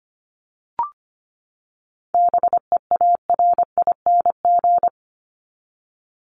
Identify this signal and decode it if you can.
Morse code, heard as a keyed tone, about 25 wpm, 700 Hz. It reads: BEARING
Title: ING Suffix - Top 100 Words with Base Words 25wpm